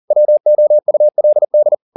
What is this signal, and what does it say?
Morse code, heard as a keyed tone, about 40 wpm, 600 Hz. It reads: WOULD
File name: the old man and the sea 2 40wpm